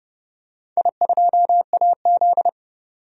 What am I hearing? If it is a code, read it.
Morse code, heard as a keyed tone, about 31 wpm, 700 Hz. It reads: I2AZ
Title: DXCC-100 Call Signs - 4 Characters - 30wpm